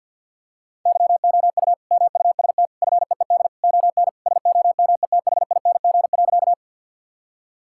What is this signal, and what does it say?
Morse code, heard as a keyed tone, about 50 wpm, 700 Hz. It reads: YOU MUST FEED ON SOMETHING.
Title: Sentences from Top 1600 Words 50wpm